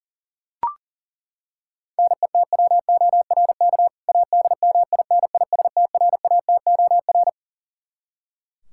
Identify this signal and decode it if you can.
Morse code, heard as a keyed tone, about 40 wpm, 700 Hz. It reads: NETWORK ADMINISTRATOR
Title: K1USN SST QSO Practice 40wpm